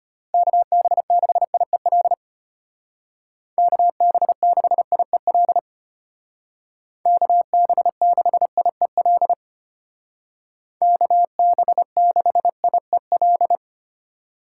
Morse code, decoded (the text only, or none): KB6IEL KB6IEL KB6IEL KB6IEL